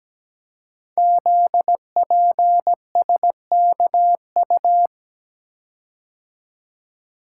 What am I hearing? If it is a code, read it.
Morse code, heard as a keyed tone, about 17 wpm, 700 Hz. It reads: ZPSKU
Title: Code Groups - 5 Letter:Numbers - 17wpm